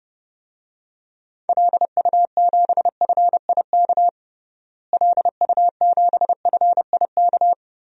Morse code, heard as a keyed tone, about 30 wpm, 700 Hz. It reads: LU7FIK LU7FIK
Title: DXCC-100 Call Signs - 6 Characters - Repeated 2x - 30wpm